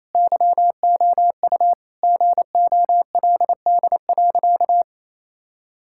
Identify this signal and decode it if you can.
Morse code, heard as a keyed tone, about 27 wpm, 700 Hz. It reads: YOU GOLD.
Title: Sentences from Top 1000 Words - Review Part 6 - 28wpm